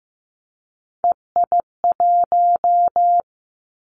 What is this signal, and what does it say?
Morse code, heard as a keyed tone, about 15 wpm, 700 Hz. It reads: EI1
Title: Callsign Prefixes for Top 50 DXCC - 15wpm